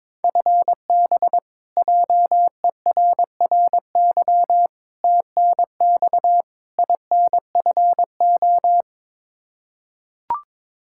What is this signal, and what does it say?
Morse code, heard as a keyed tone, about 22 wpm, 700 Hz. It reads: FB JERRY TNX INFO
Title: K1USN SST QSO Practice 22wpm